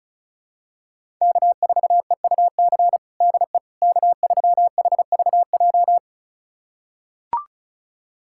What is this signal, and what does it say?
Morse code, heard as a keyed tone, about 36 wpm, 700 Hz. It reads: K4EUC DE K3HVJ
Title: USA Callsign DE Callsign - 5 Characters - 35wpm